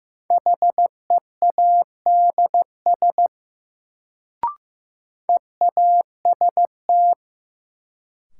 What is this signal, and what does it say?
Morse code, heard as a keyed tone, about 15 wpm, 700 Hz. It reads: HEADS EAST
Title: Sentences from Top 1000 Words - Review Part 3 - New Word Drill - 15wpm